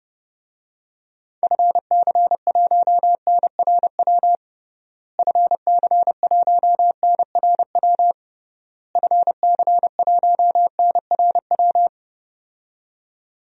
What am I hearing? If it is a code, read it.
Morse code, heard as a keyed tone, about 32 wpm, 700 Hz. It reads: FC1NRW FC1NRW FC1NRW
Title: DXCC-100 Call Signs - 6 Characters - Repeated 3x - 30wpm